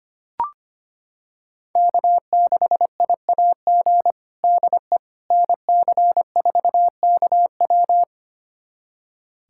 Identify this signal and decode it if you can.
Morse code, heard as a keyed tone, about 25 wpm, 700 Hz. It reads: K6IAG DE NC4KW
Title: USA Callsign DE Callsign - 5 Characters - 25wpm